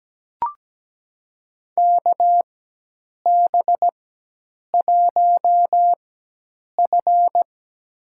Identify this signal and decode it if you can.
Morse code, heard as a keyed tone, about 17 wpm, spaced as for 8 wpm, 700 Hz. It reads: KB1F
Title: USA Call Signs - 4 Characters - 4x Character Spacing - 17wpm